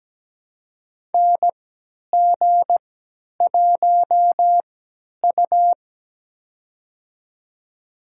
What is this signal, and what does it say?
Morse code, heard as a keyed tone, about 17 wpm, spaced as for 10 wpm, 700 Hz. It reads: NG1U
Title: USA Call Signs - 4 Characters - 3x Character Spacing - 17wpm